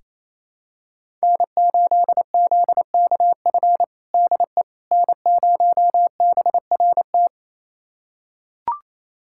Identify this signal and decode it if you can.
Morse code, heard as a keyed tone, about 27 wpm, 700 Hz. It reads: N8ZKF DE N0BRT
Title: USA Callsign DE Callsign - 5 Characters - 28wpm